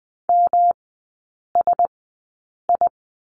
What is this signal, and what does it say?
Morse code, heard as a keyed tone, about 20 wpm, spaced as for 8 wpm, 700 Hz. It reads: MSI
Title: 127 Lesson 21e - Number 6 - Three Characters 20wpm